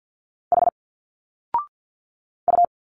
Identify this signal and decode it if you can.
Morse code, heard as a keyed tone, about 46 wpm, 700 Hz. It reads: H U
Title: Single Letters 50wpm